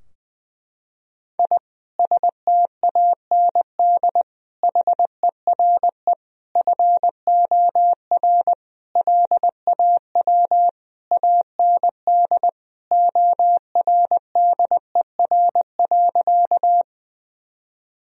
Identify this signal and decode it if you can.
Morse code, heard as a keyed tone, about 20 wpm, 700 Hz. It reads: I STAND HERE FOR LAW AND ORDER.
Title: Sentences from Top 1000 Words - Review Part 3 - 20wpm